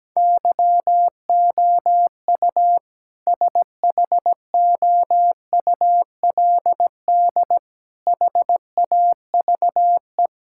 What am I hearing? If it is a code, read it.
Morse code, heard as a keyed tone, about 17 wpm, 700 Hz. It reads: YOU SHOULD HAVE
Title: Sentences from Top 1600 Words 17wpm